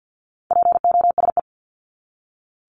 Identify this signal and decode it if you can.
Morse code, heard as a keyed tone, about 50 wpm, 700 Hz. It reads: POSE
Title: Top 500 Words 50wpm